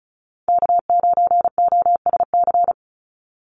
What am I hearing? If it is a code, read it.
Morse code, heard as a keyed tone, about 34 wpm, 700 Hz. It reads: K9OSC
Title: NA SOTA Chaser Callsigns 35wpm